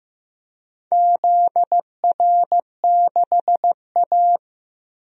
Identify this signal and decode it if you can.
Morse code, heard as a keyed tone, about 15 wpm, 700 Hz. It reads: ZR6A
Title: DXCC-100 Call Signs - 4 Characters - 15wpm